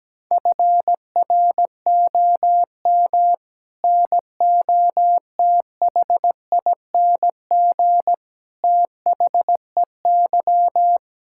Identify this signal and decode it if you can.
Morse code, heard as a keyed tone, about 17 wpm, 700 Hz. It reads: FROM NOTHING THEY
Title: Sentences from Top 1000 Words - Review Part 5 - 17wpm